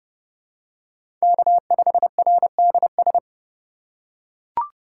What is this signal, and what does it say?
Morse code, heard as a keyed tone, about 31 wpm, 700 Hz. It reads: K5RDS